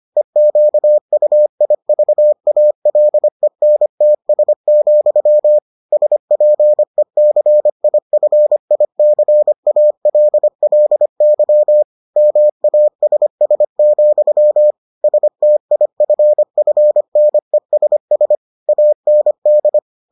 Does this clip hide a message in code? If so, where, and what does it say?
Yes: Morse code from 0.2 to 19.8 s, heard as a keyed tone, about 25 wpm, 600 Hz. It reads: EQUIVALENTS, SPECIFICALLY MASS, STIFFNESS AND